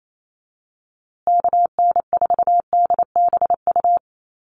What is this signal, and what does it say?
Morse code, heard as a keyed tone, about 28 wpm, 700 Hz. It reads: KN4DBU
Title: USA Call Signs Repeated 3x 28wpm